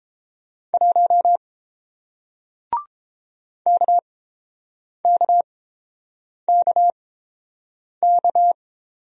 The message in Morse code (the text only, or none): 1 KKKK